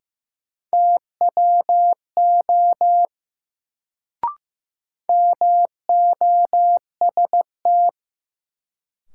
Morse code, heard as a keyed tone, about 15 wpm, 700 Hz. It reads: TWO MOST
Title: Sets of 2 Words - New Word Drill - 15wpm